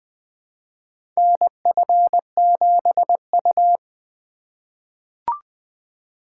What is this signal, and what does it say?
Morse code, heard as a keyed tone, about 20 wpm, 700 Hz. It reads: NF7U
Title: USA Call Signs - 3x Character Spacing - 20wpm